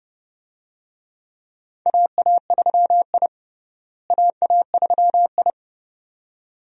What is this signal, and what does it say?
Morse code, heard as a keyed tone, about 30 wpm, 700 Hz. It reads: AA3I AA3I
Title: USA Call Signs - 4 Characters - Repeated 2x - 30wpm